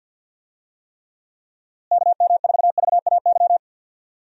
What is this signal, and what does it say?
Morse code, heard as a keyed tone, about 50 wpm, 700 Hz. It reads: KM4VAY
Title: USA Call Signs - 6 Characters - Repeated 2x - 50wpm